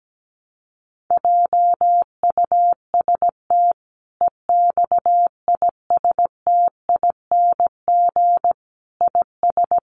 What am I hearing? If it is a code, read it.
Morse code, heard as a keyed tone, about 17 wpm, 700 Hz. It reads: JUST EXISTING IS
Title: Sentences from Top 1500 Words 17wpm